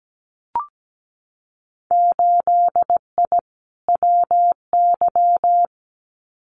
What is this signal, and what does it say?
Morse code, heard as a keyed tone, about 17 wpm, 700 Hz. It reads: 8I WY
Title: Winter Field Day Exchanges Encore 17wpm